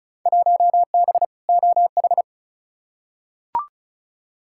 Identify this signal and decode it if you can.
Morse code, heard as a keyed tone, about 34 wpm, 700 Hz. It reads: 1B OH
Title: Field Day Exchanges - 35wpm